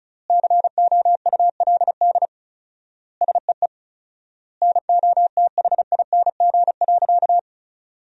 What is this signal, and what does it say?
Morse code, heard as a keyed tone, about 36 wpm, 700 Hz. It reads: COULD SEE NOTHING.
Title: 4x Word-Spacing - Sentences from Top 500 Words - 35wpm